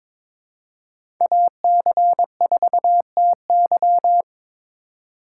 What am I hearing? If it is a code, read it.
Morse code, heard as a keyed tone, about 22 wpm, 700 Hz. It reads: AC4TY
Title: USA Call Signs - 5 Characters - 2x Character Spacing - 22wpm